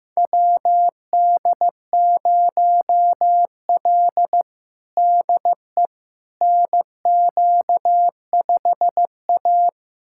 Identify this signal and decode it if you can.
Morse code, heard as a keyed tone, about 15 wpm, 700 Hz. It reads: WD0L DE NQ5A
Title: USA Callsign DE Callsign - 4 Characters - 15wpm